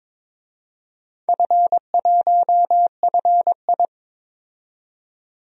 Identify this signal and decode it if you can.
Morse code, heard as a keyed tone, about 22 wpm, 700 Hz. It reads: F1FI
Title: DXCC-100 Call Signs - 4 Characters - 22wpm